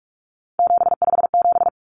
Morse code, heard as a keyed tone, about 44 wpm, 700 Hz. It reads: 757
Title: Common Cut Numbers - 3-Digit 45wpm